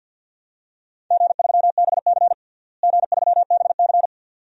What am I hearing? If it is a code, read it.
Morse code, heard as a keyed tone, about 50 wpm, 700 Hz. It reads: G3BC G3BC
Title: DXCC-100 Call Signs - 4 Characters - Repeated 2x - 50wpm